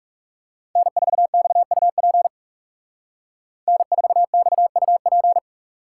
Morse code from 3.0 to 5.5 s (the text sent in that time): N4XUP